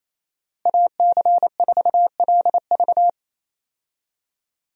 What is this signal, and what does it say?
Morse code, heard as a keyed tone, about 28 wpm, 700 Hz. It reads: AC4LV
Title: USA Call Signs - 5 Characters - Repeated 2x - 28wpm